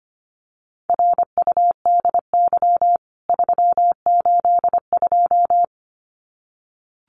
Morse code, heard as a keyed tone, about 25 wpm, 700 Hz. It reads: RUDY 382
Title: CWT Exchange 25wpm